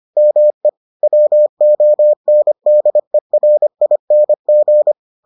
Morse code, heard as a keyed tone, about 25 wpm, 600 Hz. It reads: ME WONDERING